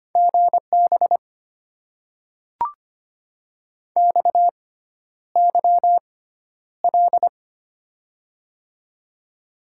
Morse code, heard as a keyed tone, about 25 wpm, spaced as for 11 wpm, 700 Hz. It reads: GB XYL
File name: Top 90 QSO Elements - 6x Character Spacing - 25wpm